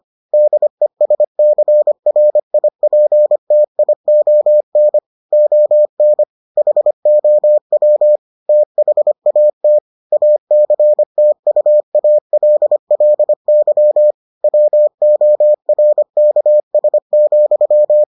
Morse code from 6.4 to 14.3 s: HOW THAT ACTUALLY

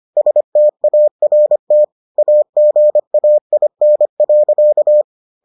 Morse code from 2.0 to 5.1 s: AGAIN.

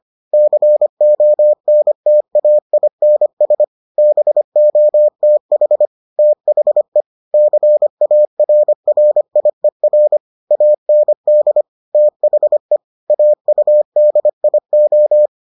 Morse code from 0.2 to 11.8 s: CONTAINS BOTH THE CARRIER AND

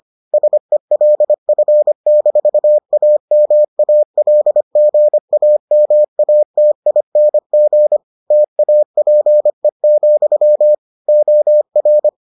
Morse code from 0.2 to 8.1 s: SELF-AMALGAMATING

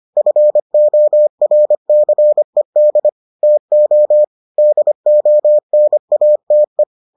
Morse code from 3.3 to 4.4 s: TO

